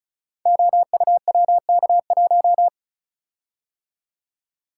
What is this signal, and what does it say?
Morse code, heard as a keyed tone, about 36 wpm, 700 Hz. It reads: OUWK1
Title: Code Groups - 5 Letter:Numbers - 35wpm